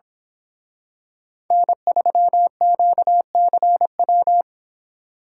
Morse code, heard as a keyed tone, about 26 wpm, 700 Hz. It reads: N3QCW
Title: Speed Racing - USA 1x3 Call Signs 22wpm